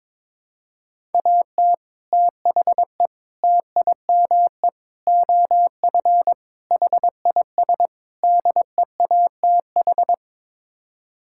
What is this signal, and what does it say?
Morse code, heard as a keyed tone, about 22 wpm, 700 Hz. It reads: AT THE TIME OF HIS DEATH
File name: practice / 22wpm / Sets of 6 Words - 22wpm